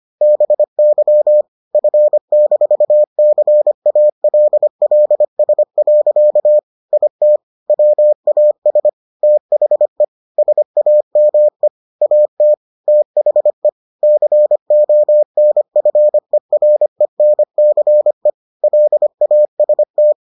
Morse code, heard as a keyed tone, about 25 wpm, 600 Hz. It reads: BY F-CALLS. IT WAS THE SAME AT THE CONFERENCE LAST